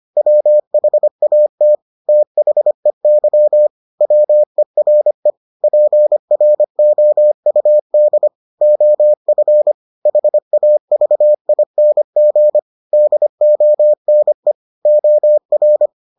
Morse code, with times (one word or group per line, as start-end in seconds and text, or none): 0.2-1.8 WHAT
2.1-3.7 THEY
4.0-5.3 WERE
5.6-8.3 PROUD
8.6-9.7 OF
10.1-12.6 HAVING
12.9-14.5 DONE
14.9-15.9 OR